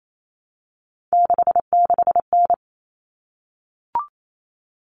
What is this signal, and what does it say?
Morse code, heard as a keyed tone, about 28 wpm, 700 Hz. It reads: 66N
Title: All Cut Numbers - 3-Digit 28wpm